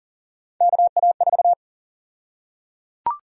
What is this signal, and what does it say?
Morse code, heard as a keyed tone, about 38 wpm, 700 Hz. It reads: KA4
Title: USA Call Sign Prefixes - Repeated 2x - 40wpm